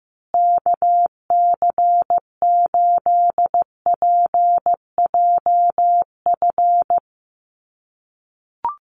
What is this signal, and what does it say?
Morse code, heard as a keyed tone, about 15 wpm, 700 Hz. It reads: KC8PJF
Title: USA Call Signs Repeated 3x 15wpm